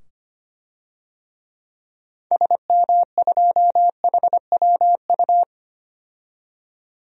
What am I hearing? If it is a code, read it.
Morse code, heard as a keyed tone, about 25 wpm, 700 Hz. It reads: SM2HWU
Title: DXCC-100 Call Signs - 6 Characters - 25wpm